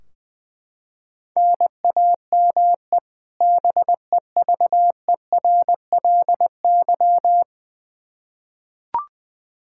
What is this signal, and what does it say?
Morse code, heard as a keyed tone, about 21 wpm, 700 Hz. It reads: NAME BEVERLY
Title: Practice QSO - Exchange 1 - Beginning - 20wpm